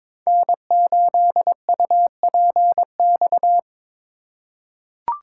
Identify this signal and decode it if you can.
Morse code, heard as a keyed tone, about 22 wpm, 700 Hz. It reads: N8UPX
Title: USA Call Signs - 6x Character Spacing - 22wpm